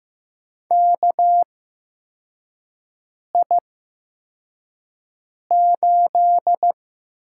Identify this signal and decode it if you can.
Morse code, heard as a keyed tone, about 15 wpm, spaced as for 4 wpm, 700 Hz. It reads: KI8
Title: USA Call Signs - 8x Character Spacing - 15wpm